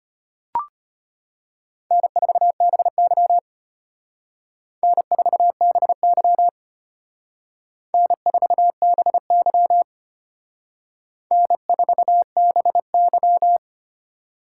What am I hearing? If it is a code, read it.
Morse code, heard as a keyed tone, about 32 wpm, 700 Hz. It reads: N4BY N4BY N4BY N4BY